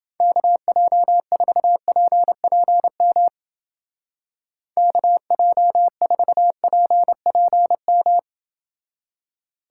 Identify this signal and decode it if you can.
Morse code, heard as a keyed tone, about 31 wpm, 700 Hz. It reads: KJ4PPM KJ4PPM